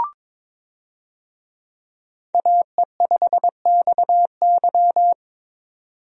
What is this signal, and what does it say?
Morse code, heard as a keyed tone, about 22 wpm, 700 Hz. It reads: AE5XY
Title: USA Call Signs - 5 Characters - 22wpm